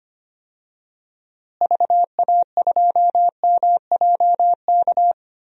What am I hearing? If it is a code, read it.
Morse code, heard as a keyed tone, about 25 wpm, 700 Hz. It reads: VA2MJK
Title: DXCC-100 Call Signs - 6 Characters - Repeated 3x - 25wpm